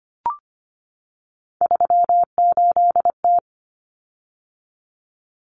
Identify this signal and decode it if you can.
Morse code, heard as a keyed tone, about 25 wpm, 700 Hz. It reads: 38T
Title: Common Cut Numbers - 3-Digit 25wpm